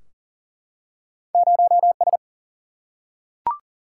Morse code, heard as a keyed tone, about 40 wpm, 700 Hz. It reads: 0S
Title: Single-Letter Single-Number 40wpm